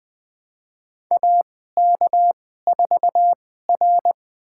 Morse code, heard as a keyed tone, about 20 wpm, spaced as for 14 wpm, 700 Hz. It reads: AK4R
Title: USA Call Signs - 4 Characters - 2x Character Spacing - 20wpm